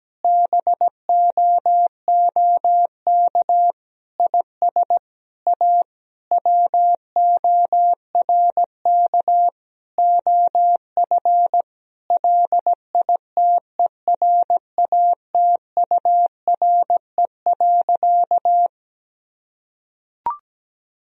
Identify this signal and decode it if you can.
Morse code, heard as a keyed tone, about 17 wpm, 700 Hz. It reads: BOOK IS A WORK OF LITERATURE.
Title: Sentences from Top 1000 Words - Review Part 3 - 17wpm